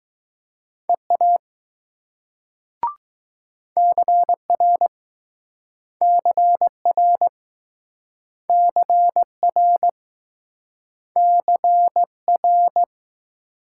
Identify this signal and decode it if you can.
Morse code, heard as a keyed tone, about 19 wpm, 700 Hz. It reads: EA CR CR CR CR